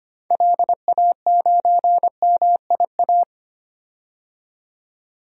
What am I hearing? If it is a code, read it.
Morse code, heard as a keyed tone, about 25 wpm, 700 Hz. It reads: LA9MIA
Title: DXCC-100 Call Signs - 6 Characters - 25wpm